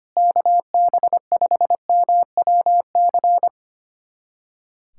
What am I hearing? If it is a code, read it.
Morse code, heard as a keyed tone, about 25 wpm, 700 Hz. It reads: KB5MWC
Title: USA Callsign DE Callsign - 6 Characters - 25wpm